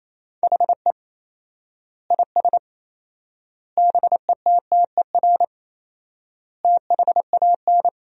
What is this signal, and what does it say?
Morse code, heard as a keyed tone, about 28 wpm, 700 Hz. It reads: HE IS BETTER THAN
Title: 4x Word-Spacing - Sentences from Top 400 Words - 28wpm